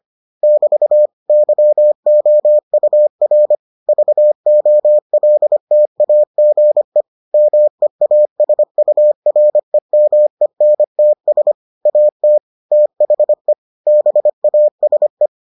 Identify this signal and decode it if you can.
Morse code, heard as a keyed tone, about 25 wpm, 600 Hz. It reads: <BT> YOUR VOLTAGE MEASUREMENTS AT THE BASE